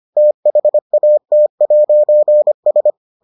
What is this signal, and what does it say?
Morse code, heard as a keyed tone, about 25 wpm, 600 Hz. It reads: THAT'S